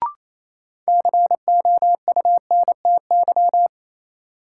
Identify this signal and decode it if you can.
Morse code, heard as a keyed tone, about 28 wpm, 700 Hz. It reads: COUNTY
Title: Sentences from Top 1200 Words - New Word Drill Alt - 28wpm